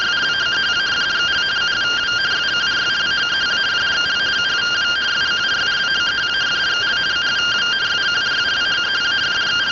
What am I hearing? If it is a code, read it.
Two-tone teleprinter signal, about 170 Hz shift.